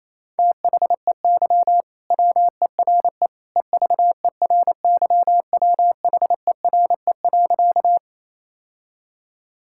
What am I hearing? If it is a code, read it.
Morse code, heard as a keyed tone, about 28 wpm, 700 Hz. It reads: THEY WERE EVERYWHERE.
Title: Sentences from Top 1600 Words 28wpm